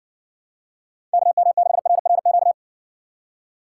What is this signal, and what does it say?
Morse code, heard as a keyed tone, about 60 wpm, 700 Hz. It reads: KM6RRQ